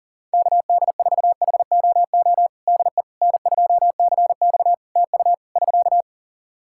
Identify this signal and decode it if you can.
Morse code, heard as a keyed tone, about 40 wpm, 700 Hz. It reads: KD4HOO DE N2CX TU <SK>
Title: Practice QSO - Exchange 4 - Ending - 40wpm